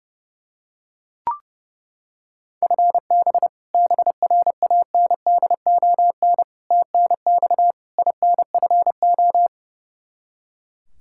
Morse code, heard as a keyed tone, about 30 wpm, 700 Hz. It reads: FB BRANDON TNX INFO
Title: Practice QSO - Exchange 3 - More Info - 30wpm